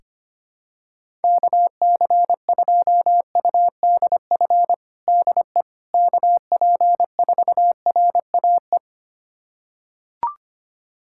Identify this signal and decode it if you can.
Morse code, heard as a keyed tone, about 25 wpm, 700 Hz. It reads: KC2UDF DE KP4RAE
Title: USA Callsign DE Callsign - 6 Characters - 25wpm